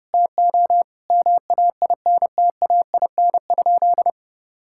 Morse code, heard as a keyed tone, about 32 wpm, 700 Hz. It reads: TO MAINTAIN?
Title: Sentences from Top 1000 Words - Review Part 5 - 30wpm